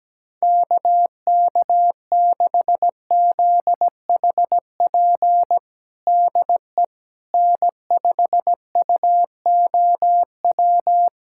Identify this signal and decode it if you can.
Morse code, heard as a keyed tone, about 17 wpm, 700 Hz. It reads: KK6ZHP DE N5UOW
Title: USA Callsign DE Callsign 17wpm